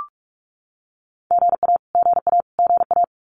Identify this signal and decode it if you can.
Morse code, heard as a keyed tone, about 44 wpm, 700 Hz. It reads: GA GA GA
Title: Abbreviations Repeated 3x - 45wpm